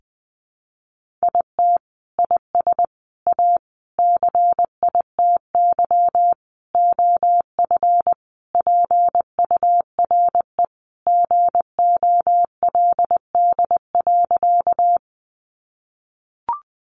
Morse code, heard as a keyed tone, about 20 wpm, 700 Hz. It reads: IT IS A CITY OF PURE GOLD.
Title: Sentences from Top 1500 Words 20wpm